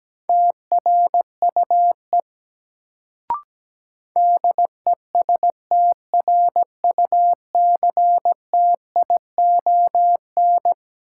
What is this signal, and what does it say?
Morse code, heard as a keyed tone, about 17 wpm, 700 Hz. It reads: TRUE DESTRUCTION
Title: Sets of 5 Words Encore - New Word Drill - 17wpm